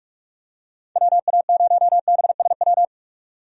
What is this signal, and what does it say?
Morse code, heard as a keyed tone, about 46 wpm, 700 Hz. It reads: WA0BSW
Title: USA Call Signs - 6 Characters - 45wpm